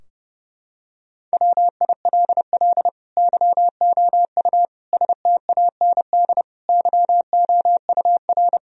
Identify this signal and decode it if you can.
Morse code, heard as a keyed tone, about 32 wpm, 700 Hz. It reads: WILL YOU STAND YOUR
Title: Sentences from Top 1000 Words - Review Part 2 - 30wpm